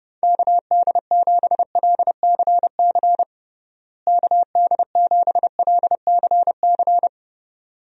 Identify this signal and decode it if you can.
Morse code, heard as a keyed tone, about 30 wpm, 700 Hz. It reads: KD7LCC KD7LCC
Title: USA Call Signs - 6 Characters - Repeated 2x - 30wpm